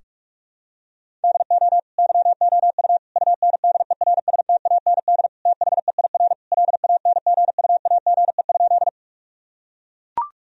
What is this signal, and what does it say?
Morse code, heard as a keyed tone, about 42 wpm, 700 Hz. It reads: DO YOU UNDERSTAND THEIR LANGUAGE?
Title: Sentences from Top 1000 Words - Review Part 3 - 45wpm